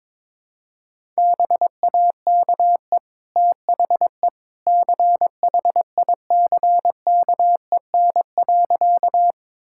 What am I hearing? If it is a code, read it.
Morse code, heard as a keyed tone, about 22 wpm, 700 Hz. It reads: BAKE THE CHICKEN.
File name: practice / 22wpm / Easy Sentences with Hints - 22wpm